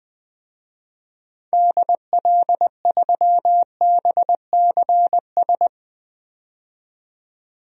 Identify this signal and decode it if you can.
Morse code, heard as a keyed tone, about 21 wpm, 700 Hz. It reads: DL3BCS
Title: DXCC-100 Call Signs - 6 Characters - 20wpm